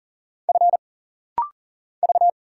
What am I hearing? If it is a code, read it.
Morse code, heard as a keyed tone, about 40 wpm, 700 Hz. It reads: F V
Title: Single Letter-Number Alt - Rapid-Fire 40wpm